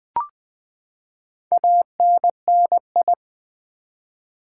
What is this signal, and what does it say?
Morse code, heard as a keyed tone, about 20 wpm, 700 Hz. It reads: ANNI
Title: Build A Word - 4 Letters At A Time - 11-Letter Words - 20wpm